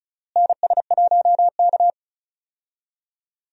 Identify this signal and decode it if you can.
Morse code, heard as a keyed tone, about 34 wpm, 700 Hz. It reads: NS1K
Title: USA Call Signs - 4 Characters - Repeated 3x - 35wpm